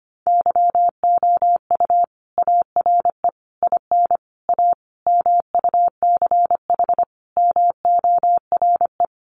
Morse code, heard as a keyed tone, about 25 wpm, 700 Hz. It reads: YOU ARE IN A MUCH MORE
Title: Sentences from Top 1400 Words 25wpm